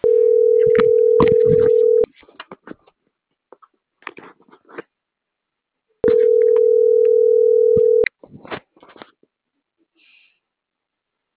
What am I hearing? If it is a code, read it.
Ringback tone (North America), 440 + 480 Hz.
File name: CENTRAL MAERSK / out luis.herrera - .27305